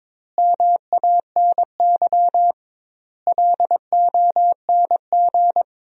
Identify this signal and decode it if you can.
Morse code, heard as a keyed tone, about 22 wpm, 700 Hz. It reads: MANY LONG
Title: 2x Word-Spacing - Sentences from Top 200 Words - 22wpm